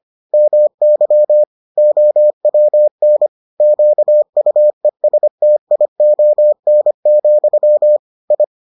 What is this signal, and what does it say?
Morse code, heard as a keyed tone, about 25 wpm, 600 Hz. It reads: MY OWN QUESTION, I